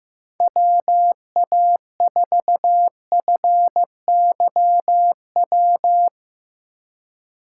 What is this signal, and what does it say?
Morse code, heard as a keyed tone, about 15 wpm, 700 Hz. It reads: WA4FYW
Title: USA Call Signs - 6 Characters - 2x Character Spacing - 15wpm